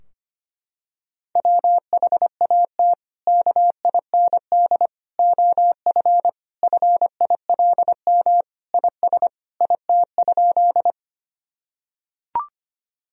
Morse code, heard as a keyed tone, about 25 wpm, 700 Hz. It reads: WHAT KIND OF FILM IS IT?
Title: Sentences from Top 1000 Words - Review Part 3 - 25wpm